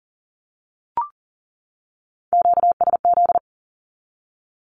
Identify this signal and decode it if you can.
Morse code, heard as a keyed tone, about 40 wpm, 700 Hz. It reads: QSZ
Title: Q-Codes 40wpm